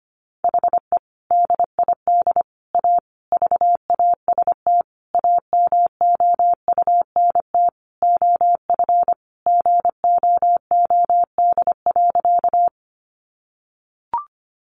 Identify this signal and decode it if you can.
Morse code, heard as a keyed tone, about 25 wpm, 700 Hz. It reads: HE DID A VAST AMOUNT OF GOOD.
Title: Sentences from Top 1300 Words 25wpm